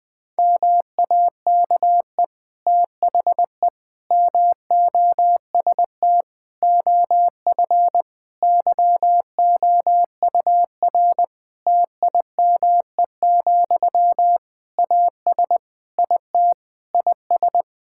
Morse code, heard as a keyed tone, about 20 wpm, 700 Hz. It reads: MAKE THE MOST OF YOUR TIME, AS IT IS
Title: Difficult Sentences with Hints - 20wpm